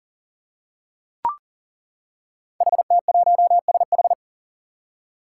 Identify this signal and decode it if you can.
Morse code, heard as a keyed tone, about 39 wpm, 700 Hz. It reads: HT1SH
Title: Five Challenging Letter-Number Combinations 40wpm